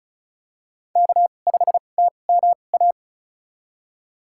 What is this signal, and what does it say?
Morse code, heard as a keyed tone, about 36 wpm, spaced as for 25 wpm, 700 Hz. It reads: K5TMA
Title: USA Call Signs - 2x Character Spacing - 35wpm